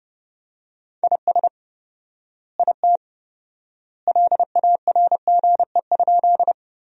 Morse code, heard as a keyed tone, about 30 wpm, 700 Hz. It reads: IS IT LARGE?
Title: 4x Word-Spacing - Sentences from Top 200 Words - 30wpm